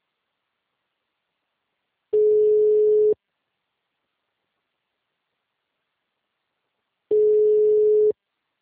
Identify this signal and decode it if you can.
Ringback tone (Europe, 425 Hz).